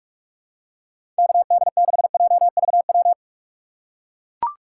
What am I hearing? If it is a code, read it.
Morse code, heard as a keyed tone, about 44 wpm, 700 Hz. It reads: KD6JVW
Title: USA Call Signs - 6 Characters - 6x Character Spacing - 45wpm